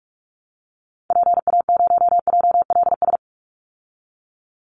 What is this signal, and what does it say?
Morse code, heard as a keyed tone, about 44 wpm, 700 Hz. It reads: PA0JLS
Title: Call Signs Repeated 2x - 45wpm